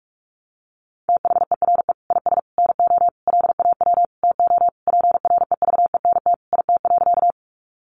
Morse code, heard as a keyed tone, about 44 wpm, 700 Hz. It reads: THERE IS NO LAW TO PREVENT IT.
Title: Sentences from Top 1100 Words 45wpm